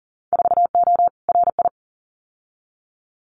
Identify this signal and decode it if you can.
Morse code, heard as a keyed tone, about 40 wpm, 700 Hz. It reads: 4O RI
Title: Winter Field Day Exchanges Encore 40wpm